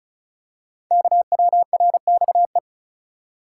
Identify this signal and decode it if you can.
Morse code, heard as a keyed tone, about 36 wpm, 690 Hz. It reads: KWRXE
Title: Code Groups - 5 Letters - 35wpm